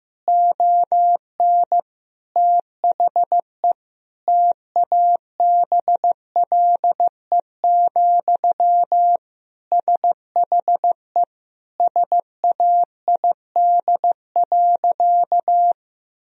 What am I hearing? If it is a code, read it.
Morse code, heard as a keyed tone, about 15 wpm, 700 Hz. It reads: ON THE TABLE, SHE SAID.
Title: Sentences from Top 1000 Words - Review Part 2 - 15wpm